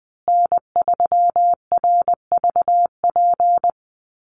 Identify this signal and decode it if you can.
Morse code, heard as a keyed tone, about 19 wpm, 700 Hz. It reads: N3RVP
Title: USA Call Signs Repeated 2x 20wpm